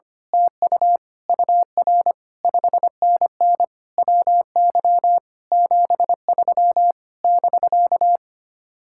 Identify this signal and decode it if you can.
Morse code, heard as a keyed tone, about 24 wpm, 700 Hz. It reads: TU UR 5NN WY 73 <BK>
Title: Speed-Racing Common QSO Phrases Encore - 25wpm